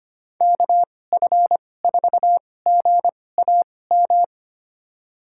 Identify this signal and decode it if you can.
Morse code, heard as a keyed tone, about 25 wpm, spaced as for 18 wpm, 700 Hz. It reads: KF4GAM
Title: USA Call Signs - 2x Character Spacing - 25wpm